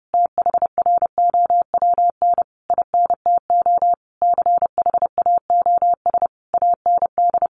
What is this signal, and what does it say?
Morse code, heard as a keyed tone, about 30 wpm, 700 Hz. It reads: THROWN INTO CHAOS AND
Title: A Sound Of Thunder 30wpm